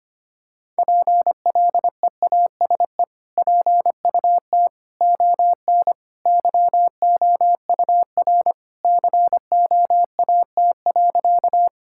Morse code, heard as a keyed tone, about 25 wpm, 700 Hz. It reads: PLEASE PUT ON YOUR COAT.
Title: Sentences from Top 1600 Words 25wpm